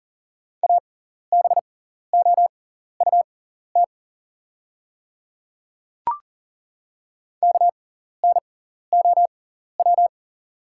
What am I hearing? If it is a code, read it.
Morse code, heard as a keyed tone, about 39 wpm, spaced as for 14 wpm, 700 Hz. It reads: ABOUT KNOW
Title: 6x Character Spacing - Top 100 Words - 40wpm